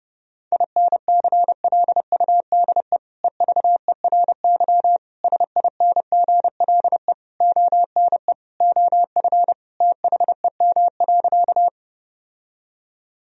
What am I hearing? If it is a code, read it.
Morse code, heard as a keyed tone, about 32 wpm, 700 Hz. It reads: INCLUDE EVERY SINGLE ONE OF THEM.
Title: Sentences from Top 1000 Words - Review Part 4 - 30wpm